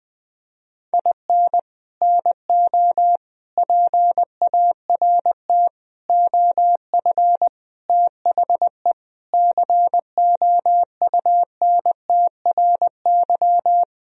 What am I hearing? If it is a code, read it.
Morse code, heard as a keyed tone, about 20 wpm, 700 Hz. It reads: IN NO PART OF THE COUNTRY
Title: Sentences from Top 1000 Words - Review Part 3 - 20wpm